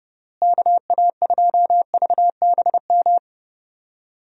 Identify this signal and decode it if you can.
Morse code, heard as a keyed tone, about 30 wpm, 700 Hz. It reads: KA2VBM
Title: USA Call Signs - 6 Characters - Repeated 2x - 30wpm